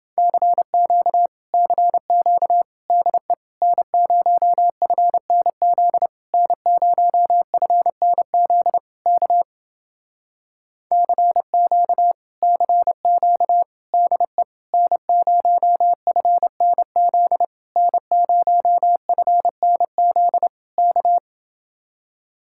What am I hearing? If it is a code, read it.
Morse code, heard as a keyed tone, about 30 wpm, 700 Hz. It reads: CQ CQ DE N0FNZ N0FNZ K CQ CQ DE N0FNZ N0FNZ K